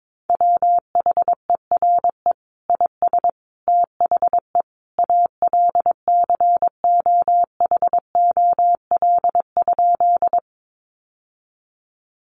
Morse code, heard as a keyed tone, about 22 wpm, 700 Hz. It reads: WHERE IS THE ALCOHOL?
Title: Sentences from Top 1400 Words 22wpm